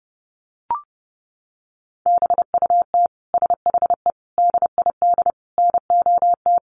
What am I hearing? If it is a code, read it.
Morse code, heard as a keyed tone, about 30 wpm, 700 Hz. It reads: BUT SHE DID NOT
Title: Sentences from Top 600 Words 30wpm